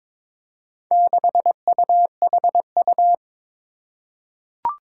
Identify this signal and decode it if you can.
Morse code, heard as a keyed tone, about 23 wpm, 700 Hz. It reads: 6UHU
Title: Four Challenging Letter-Number Combinations 22wpm